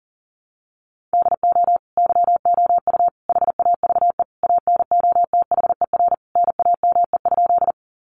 Morse code, heard as a keyed tone, about 40 wpm, 700 Hz. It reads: DO YOU HAVE ANOTHER NAME?
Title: Sentences from Top 600 Words 40wpm